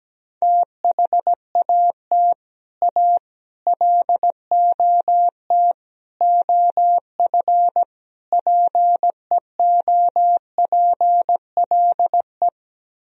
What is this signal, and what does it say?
Morse code, heard as a keyed tone, about 17 wpm, 700 Hz. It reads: THAT A LOT OF PEOPLE
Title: Speed-Racing - Sets of 5 Words 17wpm